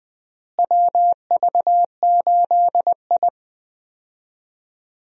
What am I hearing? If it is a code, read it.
Morse code, heard as a keyed tone, about 21 wpm, 700 Hz. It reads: WV8I
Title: USA Call Signs - 4 Characters - Repeated 3x - 20wpm